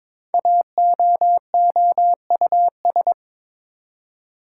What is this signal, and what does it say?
Morse code, heard as a keyed tone, about 22 wpm, 700 Hz. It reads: AOOUS
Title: Code Groups - 5 Letters - 22wpm